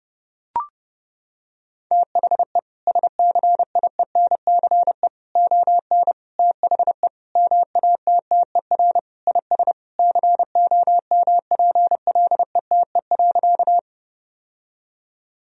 Morse code, heard as a keyed tone, about 30 wpm, 700 Hz. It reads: THE SCIENCE ON THE MATTER IS COMPLETE.
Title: Sentences from Top 1000 Words - Review Part 2 - 30wpm